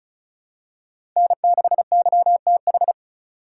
Morse code, heard as a keyed tone, about 36 wpm, 700 Hz. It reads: N6YTH
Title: USA Call Signs - 5 Characters - 4x Character Spacing - 35wpm